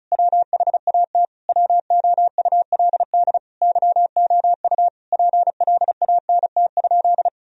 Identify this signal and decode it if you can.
Morse code, heard as a keyed tone, about 36 wpm, 700 Hz. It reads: WHAT WOULD YOU PLANT?
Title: Sentences from Top 1000 Words - Review Part 1 - 35wpm